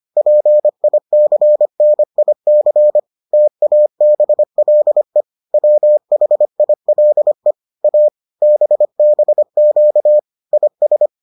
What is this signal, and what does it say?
Morse code, heard as a keyed tone, about 25 wpm, 600 Hz. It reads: PICNIC TABLE WHILE A BBQ IS